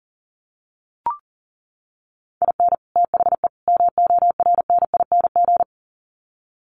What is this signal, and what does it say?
Morse code, heard as a keyed tone, about 40 wpm, 700 Hz. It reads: IN THE MORNING
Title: Sets of 3 Words 40wpm